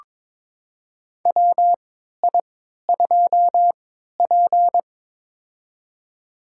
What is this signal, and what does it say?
Morse code, heard as a keyed tone, about 23 wpm, spaced as for 13 wpm, 700 Hz. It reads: WI2P
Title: USA Call Signs - 4 Characters - 3x Character Spacing - 22wpm